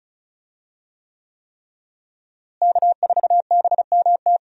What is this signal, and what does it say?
Morse code, heard as a keyed tone, about 36 wpm, 700 Hz. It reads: K4BMT
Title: USA Call Signs - 5 Characters - Repeated 3x - 35wpm